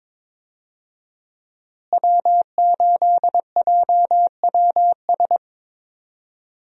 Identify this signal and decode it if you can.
Morse code, heard as a keyed tone, about 22 wpm, 700 Hz. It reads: W8JWS
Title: USA Call Signs - 5 Characters - Repeated 2x - 22wpm